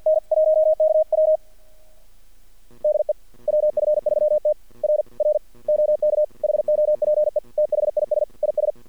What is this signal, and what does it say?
Morse code, heard as a keyed tone, about 48 wpm, 620 Hz. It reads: N1KW DE WF3T R R OK HOPE THIS IS
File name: archived / CRSnet